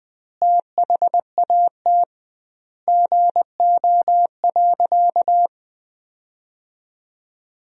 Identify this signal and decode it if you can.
Morse code, heard as a keyed tone, about 20 wpm, 700 Hz. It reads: THAT GO.
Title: 2x Word-Spacing - Sentences from Top 300 Words - 20wpm